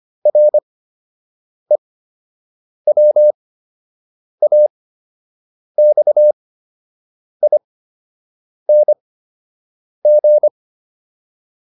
Morse code, heard as a keyed tone, about 25 wpm, spaced as for 7 wpm, 600 Hz. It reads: REWAXING
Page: fundamental-practice-files